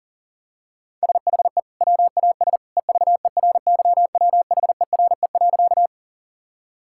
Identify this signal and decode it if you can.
Morse code, heard as a keyed tone, about 40 wpm, 700 Hz. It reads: SHE WAS EVERYWHERE.
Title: Sentences from Top 1600 Words 40wpm